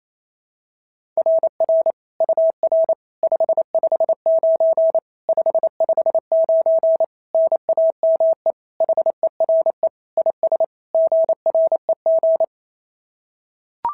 Morse code, heard as a keyed tone, about 27 wpm, 650 Hz. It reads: RR UR 559 559 NAME HERE IS GREG
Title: SOTA QSO Practice 28wpm